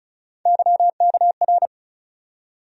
Morse code, heard as a keyed tone, about 36 wpm, 700 Hz. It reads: YKR
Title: USA Call Sign Postfixes - 3 Letters 35wpm